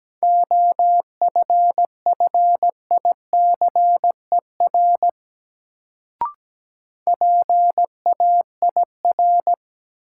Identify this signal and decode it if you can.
Morse code, heard as a keyed tone, about 17 wpm, 700 Hz. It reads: OFFICER PAIR
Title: Sentences from Top 1200 Words - New Word Drill Alt - 17wpm